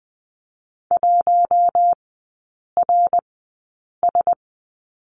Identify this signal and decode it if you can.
Morse code, heard as a keyed tone, about 20 wpm, spaced as for 8 wpm, 700 Hz. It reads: 1RS
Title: 053 Lesson 10e - Letter r - Three Characters 20wpm